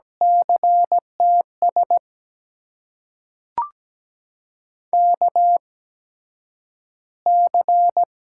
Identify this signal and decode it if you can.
Morse code, heard as a keyed tone, about 17 wpm, spaced as for 9 wpm, 700 Hz. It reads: CTS KC
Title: USA Call Signs - 6 Characters - 8x Character Spacing - 17wpm